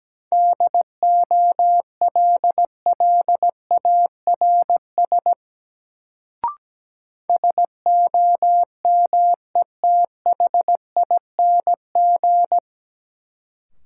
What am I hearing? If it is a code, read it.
Morse code, heard as a keyed tone, about 17 wpm, 700 Hz. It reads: DOLLARS SOMETHING
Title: Sets of 5 Words - New Word Drill - 17wpm